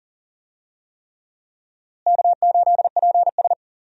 Morse code, heard as a keyed tone, about 40 wpm, 700 Hz. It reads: K8PS
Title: USA Call Signs - 4 Characters - Repeated 3x - 40wpm